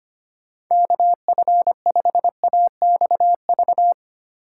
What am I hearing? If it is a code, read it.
Morse code, heard as a keyed tone, about 25 wpm, 700 Hz. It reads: KF5AXV
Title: USA Call Signs - 6 Characters - 3x Character Spacing - 25wpm